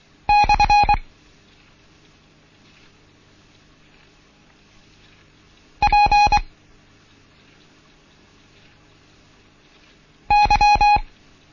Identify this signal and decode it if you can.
Morse code, heard as a keyed tone, about 24 wpm, 830 Hz. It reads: / P Y
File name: Morsen / Kurs A 16